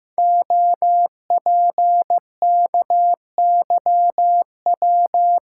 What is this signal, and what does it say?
Morse code, heard as a keyed tone, about 15 wpm, 700 Hz. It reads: OPKYW
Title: Code Groups - 5 Letter:Numbers - 15wpm